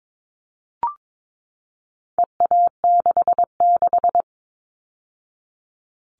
Morse code, heard as a keyed tone, about 22 wpm, 700 Hz. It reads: EA66
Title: All Cut Numbers - 4-Digit 22wpm